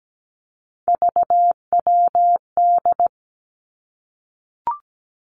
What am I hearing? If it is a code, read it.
Morse code, heard as a keyed tone, about 17 wpm, 700 Hz. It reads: VWD
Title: Three Challenging Letter-Number Combinations 17wpm